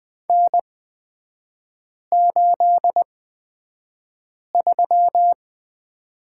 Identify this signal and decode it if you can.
Morse code, heard as a keyed tone, about 20 wpm, spaced as for 5 wpm, 700 Hz. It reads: N83
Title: 224 Lesson 36e - BT - Three Characters 20wpm